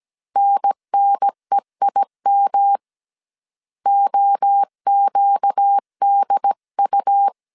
Morse code, heard as a keyed tone, about 17 wpm, 800 Hz. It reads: NNEIM OQDU
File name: CW / DET / 1 6 1500C S A 16wpm 2635